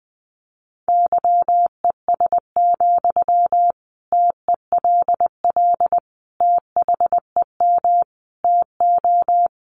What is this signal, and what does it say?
Morse code, heard as a keyed tone, about 20 wpm, 700 Hz. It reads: YES, TELL THEM TO
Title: Sentences from Top 500 Words 20wpm